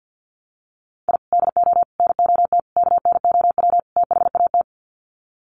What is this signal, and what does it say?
Morse code, heard as a keyed tone, about 50 wpm, 700 Hz. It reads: I DO NOT KNOW THAT
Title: Sets of 5 Words 50wpm